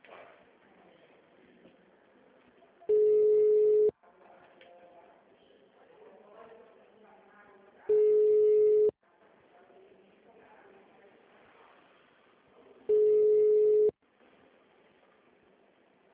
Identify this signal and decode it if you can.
Ringback tone (Europe, 425 Hz), with speech.
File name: out- -2012- - - .103